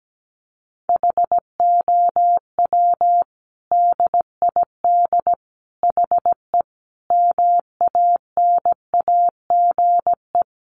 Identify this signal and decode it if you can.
Morse code, heard as a keyed tone, about 17 wpm, 700 Hz. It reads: HOW DID HE MANAGE
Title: Sentences from Top 1400 Words 17wpm